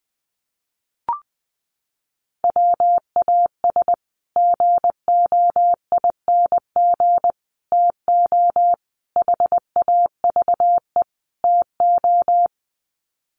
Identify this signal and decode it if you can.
Morse code, heard as a keyed tone, about 20 wpm, 700 Hz. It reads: WAS GOING TO HAVE TO
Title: Sets of 5 Words Encore 20wpm